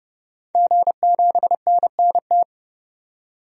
Morse code, heard as a keyed tone, about 30 wpm, 700 Hz. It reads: G7NNT
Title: DXCC-100 Call Signs - 5 Characters - 30wpm